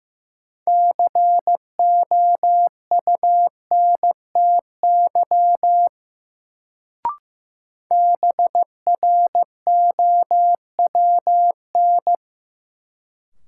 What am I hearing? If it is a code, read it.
Morse code, heard as a keyed tone, about 15 wpm, 700 Hz. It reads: COUNTY BROWN
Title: Sentences from Top 1200 Words - New Word Drill Alt - 15wpm